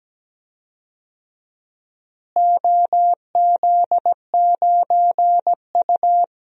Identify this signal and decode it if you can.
Morse code, heard as a keyed tone, about 17 wpm, 700 Hz. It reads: OZ9U
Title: DXCC-100 Call Signs - 4 Characters - Repeated 2x - 17wpm